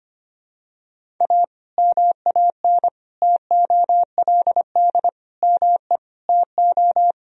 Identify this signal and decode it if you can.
Morse code, heard as a keyed tone, about 25 wpm, 700 Hz. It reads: A MAN TOLD ME TO
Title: Sentences from Top 1000 Words - Review Part 3 - 25wpm